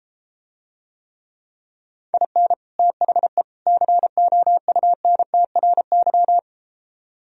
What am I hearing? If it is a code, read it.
Morse code, heard as a keyed tone, about 33 wpm, 700 Hz. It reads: IN THE COUNTRY